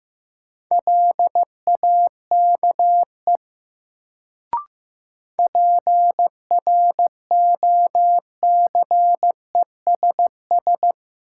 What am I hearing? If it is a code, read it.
Morse code, heard as a keyed tone, about 15 wpm, 700 Hz. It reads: LAKE PROCESS
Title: Sentences from Top 1000 Words - Review Part 3 - New Word Drill - 15wpm